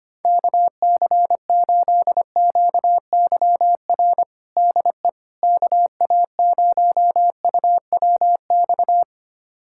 Morse code, heard as a keyed tone, about 25 wpm, 700 Hz. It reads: KC8QYR DE KA0UWX
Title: USA Callsign DE Callsign - 6 Characters - 25wpm